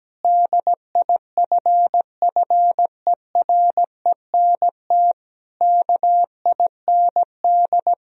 Morse code, heard as a keyed tone, about 17 wpm, 700 Hz. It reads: DIFFERENT KIND
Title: Sentences from Top 1000 Words - Review Part 4 - 17wpm